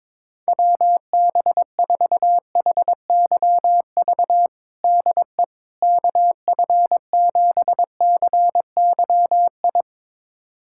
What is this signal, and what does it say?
Morse code, heard as a keyed tone, about 22 wpm, 700 Hz. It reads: WB4HYV DE KF7CYI
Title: USA Callsign DE Callsign - 6 Characters - 22wpm